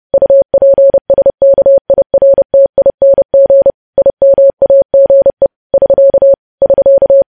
Morse code, heard as a keyed tone, about 30 wpm, 550 Hz. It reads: UPSKIRTING IMAGE <SK> <SK>
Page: news headlines in morse code at 30 wpm